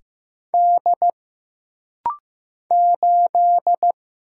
Morse code, heard as a keyed tone, about 15 wpm, 700 Hz. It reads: D 8
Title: Single Letter-Number Alt - Rapid-Fire 15wpm